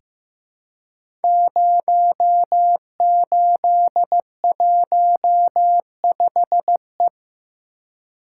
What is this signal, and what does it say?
Morse code, heard as a keyed tone, about 15 wpm, 700 Hz. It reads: 0815E
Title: Code Groups - 5 Letter:Numbers - 15wpm